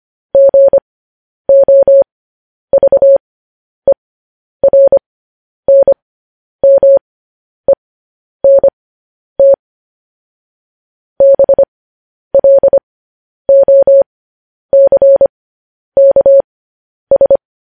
Morse code, heard as a keyed tone, about 25 wpm, spaced as for 10 wpm, 550 Hz. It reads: GOVERNMENT BLOCKS